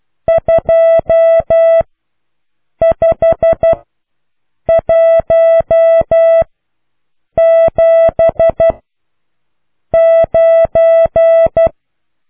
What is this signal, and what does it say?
Morse code, heard as a keyed tone, about 12 wpm, spaced as for 6 wpm, 660 Hz. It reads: 25179